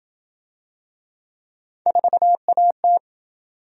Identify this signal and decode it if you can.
Morse code, heard as a keyed tone, about 27 wpm, 700 Hz. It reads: 4AT